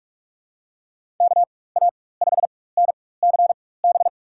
Morse code, heard as a keyed tone, about 44 wpm, spaced as for 21 wpm, 700 Hz. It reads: KA5NCB
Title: USA Call Signs - 6 Characters - 4x Character Spacing - 45wpm